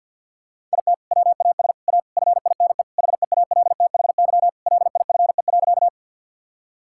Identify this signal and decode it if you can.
Morse code, heard as a keyed tone, about 51 wpm, 700 Hz. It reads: IT WAS A FINE HEALTHY LIFE.
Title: Sentences from Top 1000 Words - Review Part 5 - 50wpm